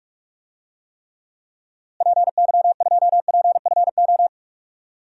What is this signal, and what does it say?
Morse code, heard as a keyed tone, about 42 wpm, 700 Hz. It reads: PY2PFO
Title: DXCC-100 Call Signs - 6 Characters - Repeated 2x - 45wpm